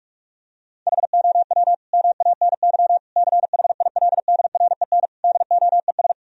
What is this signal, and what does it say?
Morse code, heard as a keyed tone, about 48 wpm, 700 Hz. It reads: HOW MANY CHILDREN DOES
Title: Sentences from Top 1000 Words - Review Part 1 - 45wpm